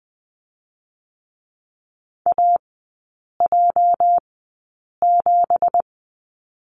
Morse code, heard as a keyed tone, about 19 wpm, 700 Hz. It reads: A J 7
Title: 205 Lesson 33e - Letter j - Three Characters 20wpm